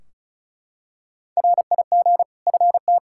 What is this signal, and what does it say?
Morse code, heard as a keyed tone, about 34 wpm, 700 Hz. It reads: RIG FT
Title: Practice QSO - Exchange 2 - Info - 35wpm